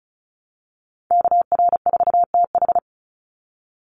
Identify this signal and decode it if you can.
Morse code, heard as a keyed tone, about 34 wpm, 700 Hz. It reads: KR4TH
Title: Call Signs - 35wpm